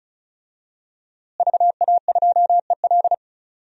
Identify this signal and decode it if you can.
Morse code, heard as a keyed tone, about 36 wpm, 700 Hz. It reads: VA2EL
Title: DXCC-100 Call Signs - 5 Characters - Repeated 2x - 35wpm